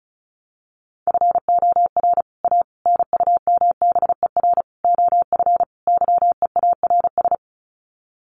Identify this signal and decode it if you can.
Morse code, heard as a keyed tone, about 34 wpm, 700 Hz. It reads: FOR A NUMBER OF YEARS
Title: Sets of 5 Words Encore 35wpm